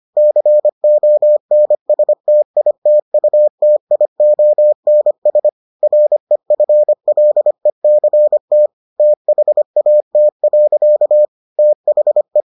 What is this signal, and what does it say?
Morse code, heard as a keyed tone, about 25 wpm, 600 Hz. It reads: CONSTITUTIONS REFLECT THAT. THE